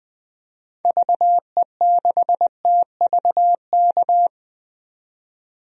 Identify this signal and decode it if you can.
Morse code, heard as a keyed tone, about 21 wpm, 700 Hz. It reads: VE6TVK
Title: DXCC-100 Call Signs - 6 Characters - 20wpm